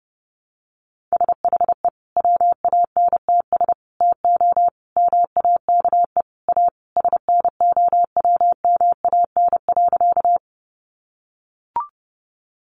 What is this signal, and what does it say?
Morse code, heard as a keyed tone, about 30 wpm, 700 Hz. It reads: SHE WANTS TO MAKE A SNOWMAN.
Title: Easy Phrases 30wpm